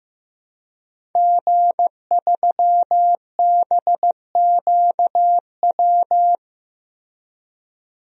Morse code, heard as a keyed tone, about 15 wpm, 700 Hz. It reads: G3BQW
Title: DXCC-100 Call Signs - 5 Characters - Repeated 2x - 15wpm